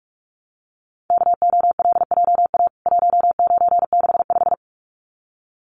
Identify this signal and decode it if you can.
Morse code, heard as a keyed tone, about 44 wpm, 700 Hz. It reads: KOLJA 1965
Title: CWT Exchange 45wpm